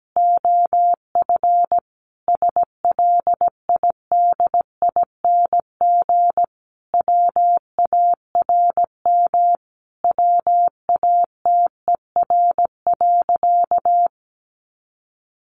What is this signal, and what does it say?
Morse code, heard as a keyed tone, about 17 wpm, 700 Hz. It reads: OF SLIDING WARM WATER.
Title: A Sound Of Thunder 17wpm